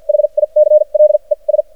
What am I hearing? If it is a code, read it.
Morse code, heard as a keyed tone, about 53 wpm, spaced as for 38 wpm, 610 Hz. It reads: HICLES
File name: archived / CRSnet